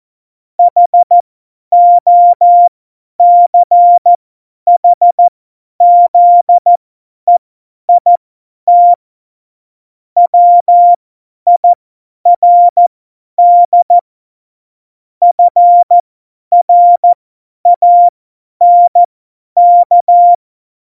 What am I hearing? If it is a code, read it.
Morse code, heard as a keyed tone, about 14 wpm, spaced as for 10 wpm, 700 Hz. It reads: HOCHZEIT WIRD FRANK